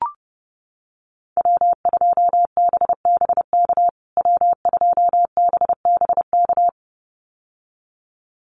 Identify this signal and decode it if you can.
Morse code, heard as a keyed tone, about 30 wpm, 700 Hz. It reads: W2BBK W2BBK
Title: USA Call Signs Repeated 2x 30wpm